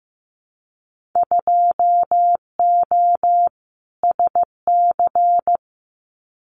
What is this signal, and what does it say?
Morse code, heard as a keyed tone, about 15 wpm, 700 Hz. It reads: 2O SC
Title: Winter Field Day Exchanges Encore 15wpm